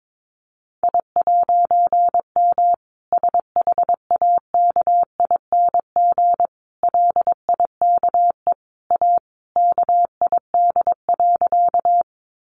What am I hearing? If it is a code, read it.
Morse code, heard as a keyed tone, about 22 wpm, 700 Hz. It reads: I'M SHAKING LIKE A KID.
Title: A Sound Of Thunder 22wpm